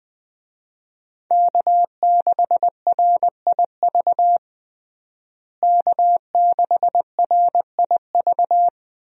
Morse code, heard as a keyed tone, about 20 wpm, 700 Hz. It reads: K6RIV K6RIV
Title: USA Call Signs - 5 Characters - Repeated 3x - 20wpm